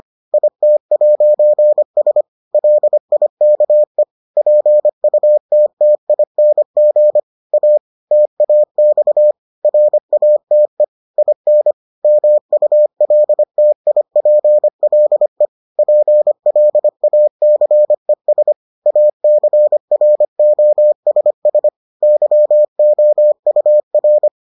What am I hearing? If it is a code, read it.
Morse code, heard as a keyed tone, about 25 wpm, 600 Hz. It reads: IT'S LIKE PUTTING A TAX RATE IN MULTIPLE PLACES ACROSS YOUR